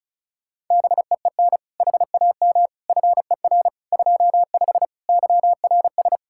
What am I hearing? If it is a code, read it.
Morse code, heard as a keyed tone, about 47 wpm, spaced as for 41 wpm, 700 Hz. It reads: BEEN HAM FER 25 YRS
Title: Speed-Racing Common QSO Phrases Encore - 35wpm